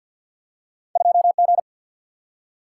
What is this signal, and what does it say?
Morse code, heard as a keyed tone, about 47 wpm, 700 Hz. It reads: 2 G
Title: Single-Letter Single-Number 50wpm